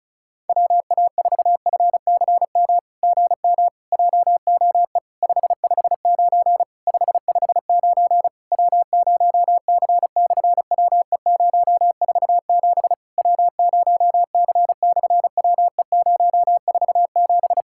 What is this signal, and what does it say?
Morse code, heard as a keyed tone, about 36 wpm, 700 Hz. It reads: WA4FCM GM JOE 559 559 W0C/WE047 W0C/WE047
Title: SOTA QSO Practice 35wpm